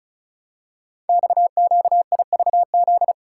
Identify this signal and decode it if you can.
Morse code, heard as a keyed tone, about 36 wpm, 690 Hz. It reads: XQIVZ
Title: Code Groups - 5 Letters - 35wpm